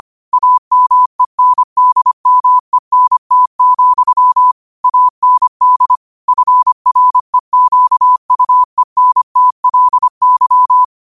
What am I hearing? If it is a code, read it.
Morse code, heard as a keyed tone, about 25 wpm, 1000 Hz. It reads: AMENDMENT, AND FREQUENTLY
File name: qotdfiles / 25wpm